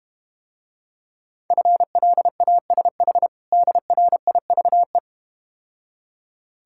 Morse code, heard as a keyed tone, about 33 wpm, 700 Hz. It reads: FLASH DRIVE